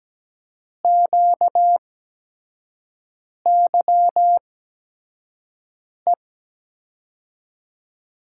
Morse code, heard as a keyed tone, about 17 wpm, spaced as for 5 wpm, 700 Hz. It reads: QYE
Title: USA Call Signs - 6 Characters - 8x Character Spacing - 17wpm